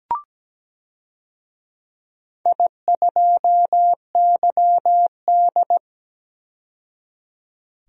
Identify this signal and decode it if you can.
Morse code, heard as a keyed tone, about 17 wpm, 700 Hz. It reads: I2YD
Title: DXCC-100 Call Signs - 4 Characters - 17wpm